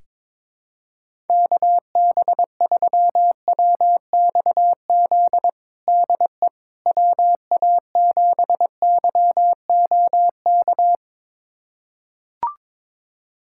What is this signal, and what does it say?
Morse code, heard as a keyed tone, about 21 wpm, 700 Hz. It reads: KB3WXZ DE WA7YOK
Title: USA Callsign DE Callsign - 6 Characters - 22wpm